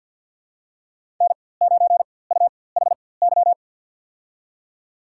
Morse code, heard as a keyed tone, about 51 wpm, spaced as for 23 wpm, 700 Hz. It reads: N9UHY
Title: USA Call Signs - 5 Characters - 4x Character Spacing - 50wpm